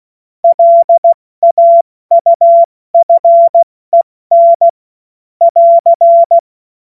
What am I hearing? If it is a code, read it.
Morse code, heard as a keyed tone, about 16 wpm, 680 Hz. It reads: LAUFEN <AR>
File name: Witz-07-Oktober-2025-70-F80